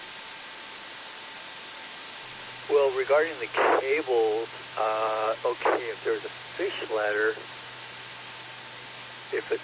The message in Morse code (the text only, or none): none